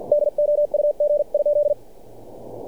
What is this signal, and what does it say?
Morse code, heard as a keyed tone, about 45 wpm, 590 Hz. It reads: DOUG?
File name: archived / CRSnet